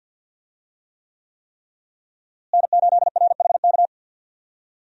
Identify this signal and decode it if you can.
Morse code, heard as a keyed tone, about 53 wpm, 700 Hz. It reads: N8RHK
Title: USA Call Signs - 5 Characters - Repeated 3x - 50wpm